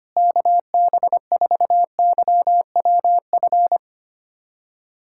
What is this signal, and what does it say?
Morse code, heard as a keyed tone, about 25 wpm, 700 Hz. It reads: KB4YWF
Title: USA Callsign DE Callsign - 6 Characters - 25wpm